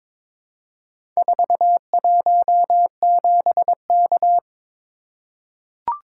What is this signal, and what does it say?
Morse code, heard as a keyed tone, about 22 wpm, 700 Hz. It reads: 417K
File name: practice / 22wpm / Four Letter-Number Combinations - 22wpm